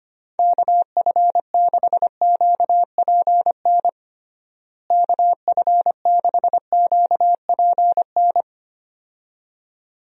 Morse code, heard as a keyed tone, about 25 wpm, 700 Hz. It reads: KF6QPN KF6QPN
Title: USA Call Signs - 6 Characters - Repeated 2x - 25wpm